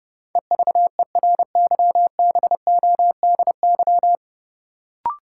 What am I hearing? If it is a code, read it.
Morse code, heard as a keyed tone, about 30 wpm, 700 Hz. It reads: EVERYBODY
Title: Sentences from Top 1400 Words - New Word Drill - 30wpm